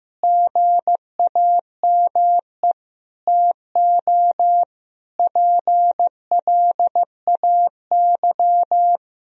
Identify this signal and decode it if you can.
Morse code, heard as a keyed tone, about 15 wpm, 700 Hz. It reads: GAME TO PLAY
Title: Sentences from Top 1000 Words - Review Part 3 - 15wpm